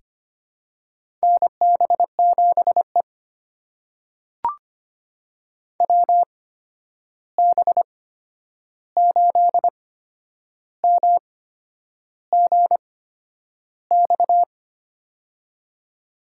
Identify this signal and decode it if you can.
Morse code, heard as a keyed tone, about 25 wpm, spaced as for 9 wpm, 700 Hz. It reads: NB7E WB8MGX
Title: USA Call Signs - 8x Character Spacing - 25wpm